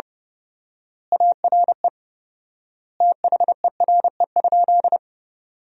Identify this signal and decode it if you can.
Morse code, heard as a keyed tone, about 30 wpm, 700 Hz. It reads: ARE THERE?
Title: 4x Word-Spacing - Sentences from Top 100 Words - 30wpm